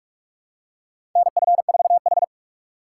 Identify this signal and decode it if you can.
Morse code, heard as a keyed tone, about 45 wpm, 700 Hz. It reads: NF4H
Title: USA Call Signs - 4 Characters - 3x Character Spacing - 45wpm